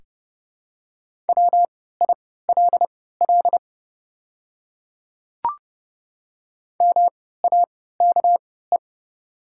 Morse code, heard as a keyed tone, about 30 wpm, spaced as for 17 wpm, 700 Hz. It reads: WILL MAKE
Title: 3x Character Spacing - Top 100 Words - 30wpm